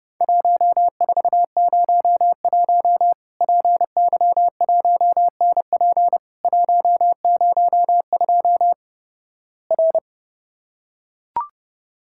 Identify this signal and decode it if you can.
Morse code, heard as a keyed tone, about 32 wpm, 700 Hz. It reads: 1401 PY1NP 102 R
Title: Worked All Europe DX Contest - QTC Exchange - 30wpm